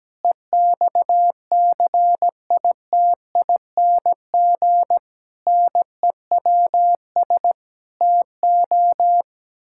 Morse code, heard as a keyed tone, about 17 wpm, 700 Hz. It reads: EXCITING NEWS TO
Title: Pangram Challenge v2 - 17wpm